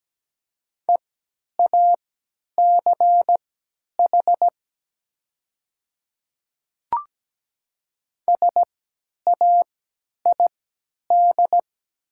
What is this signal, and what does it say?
Morse code, heard as a keyed tone, about 17 wpm, spaced as for 10 wpm, 700 Hz. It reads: EACH SAID
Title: 3x Character Spacing - Top 100 Words - 17wpm